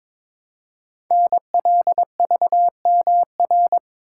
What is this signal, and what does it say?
Morse code, heard as a keyed tone, about 22 wpm, 700 Hz. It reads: NLVMR
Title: Code Groups - 5 Letters - 22wpm